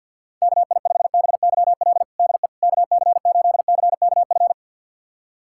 Morse code, heard as a keyed tone, about 50 wpm, 700 Hz. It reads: KI5BYL DE KC8CKF
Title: USA Callsign DE Callsign - 6 Characters - 50wpm